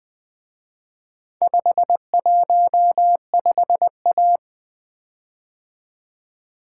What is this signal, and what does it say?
Morse code, heard as a keyed tone, about 24 wpm, 700 Hz. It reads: 515A
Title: Speed-Racing - Four Letter-Number Combinations - 20wpm